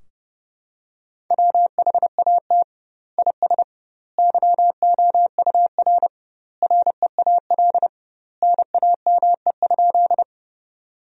Morse code, heard as a keyed tone, about 32 wpm, 700 Hz. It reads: WHAT IS YOUR REAL NAME?
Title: 2x Word-Spacing - Sentences from Top 300 Words - 30wpm